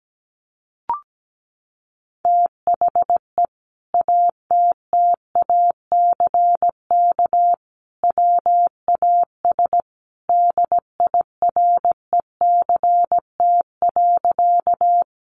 Morse code, heard as a keyed tone, about 17 wpm, 700 Hz. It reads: THE ATTACK WAS DIRECT.
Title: Sentences from Top 1500 Words 17wpm